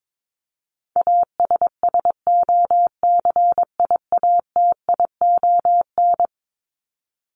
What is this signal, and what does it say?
Morse code, heard as a keyed tone, about 22 wpm, 700 Hz. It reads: ASSOCIATION
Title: Top 1001-1250 Words 22wpm